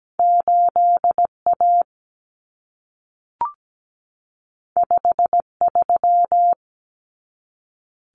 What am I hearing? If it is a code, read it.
Morse code, heard as a keyed tone, about 17 wpm, 700 Hz. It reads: 8 A 5 3
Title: Common Cut Numbers - 2-Digit 17wpm